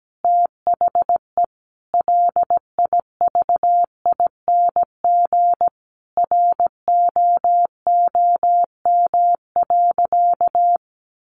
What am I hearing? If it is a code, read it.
Morse code, heard as a keyed tone, about 17 wpm, 700 Hz. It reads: THE LIVING ROOM.
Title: Sentences from Top 1500 Words 17wpm